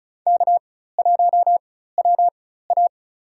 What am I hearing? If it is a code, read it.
Morse code, heard as a keyed tone, about 36 wpm, spaced as for 16 wpm, 700 Hz. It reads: K1WA
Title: USA Call Signs - 4x Character Spacing - 35wpm